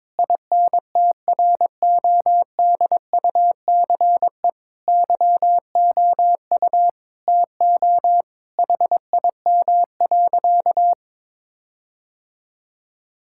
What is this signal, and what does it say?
Morse code, heard as a keyed tone, about 22 wpm, 700 Hz. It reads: INTRODUCE YOU TO HIM.
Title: Sentences from Top 1600 Words 22wpm